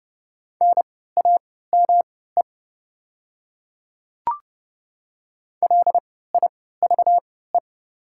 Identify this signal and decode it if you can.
Morse code, heard as a keyed tone, about 30 wpm, spaced as for 17 wpm, 700 Hz. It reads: NAME LIVE
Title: 3x Character Spacing - Top 200 Words - 30wpm